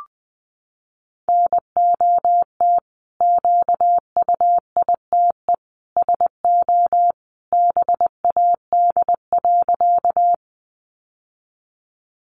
Morse code, heard as a keyed tone, about 20 wpm, 700 Hz. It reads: NOT QUITE SO BAD.
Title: Sentences from Top 700 Words 20wpm